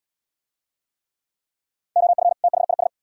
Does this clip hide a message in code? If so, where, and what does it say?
Yes: Morse code from 2.0 to 2.9 s, heard as a keyed tone, about 66 wpm, 700 Hz. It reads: OF THIS